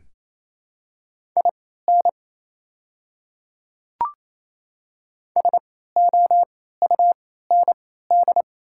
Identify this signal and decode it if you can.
Morse code, heard as a keyed tone, about 27 wpm, spaced as for 16 wpm, 700 Hz. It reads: IN SOUND
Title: 3x Character Spacing - Top 200 Words - 28wpm